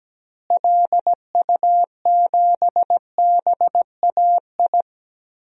Morse code, heard as a keyed tone, about 17 wpm, 700 Hz. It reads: LU7BAI
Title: DXCC-100 Call Signs - 6 Characters - Repeated 2x - 17wpm